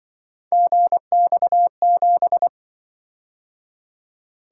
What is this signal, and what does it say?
Morse code, heard as a keyed tone, about 23 wpm, 700 Hz. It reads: GX7
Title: Speed-Racing - Three Letter-Number Combinations - 20wpm